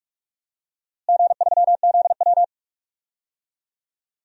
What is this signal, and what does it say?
Morse code, heard as a keyed tone, about 46 wpm, 700 Hz. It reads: G3ZW
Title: DXCC-100 Call Signs - 4 Characters - Repeated 3x - 45wpm